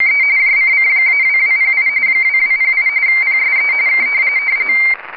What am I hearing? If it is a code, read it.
RTTY, about 45.45 baud. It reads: E JCC-2208 2208 PSE K